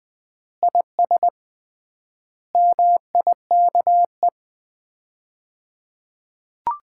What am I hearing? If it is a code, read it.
Morse code, heard as a keyed tone, about 21 wpm, 700 Hz. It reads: IS MIKE
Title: Common QSO Phrases - 3x Word-Spacing - 20wpm